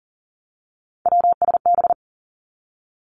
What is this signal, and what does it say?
Morse code, heard as a keyed tone, about 41 wpm, 700 Hz. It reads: WSB
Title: Three Challenging Letter-Number Combinations 40wpm